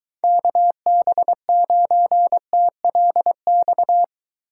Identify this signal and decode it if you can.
Morse code, heard as a keyed tone, about 23 wpm, 700 Hz. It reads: KB9TLX